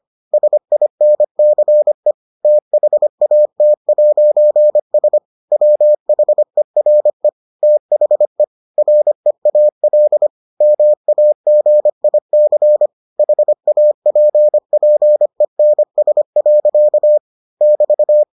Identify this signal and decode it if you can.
Morse code, heard as a keyed tone, about 25 wpm, 600 Hz. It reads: SINCE THAT'S WHERE THE REAL MAGIC HAPPENS. <BT>